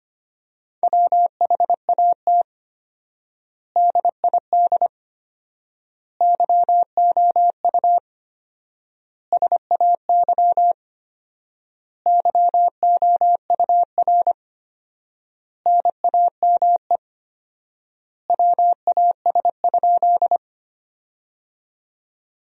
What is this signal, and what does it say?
Morse code, heard as a keyed tone, about 25 wpm, 700 Hz. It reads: WHAT DID YOU SAY YOUR NAME WAS?
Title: 4x Word-Spacing - Sentences from Top 200 Words - 25wpm